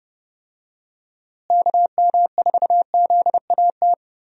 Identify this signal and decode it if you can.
Morse code, heard as a keyed tone, about 32 wpm, 700 Hz. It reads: KM4ZAT
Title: USA Call Signs - 6 Characters - Repeated 3x - 30wpm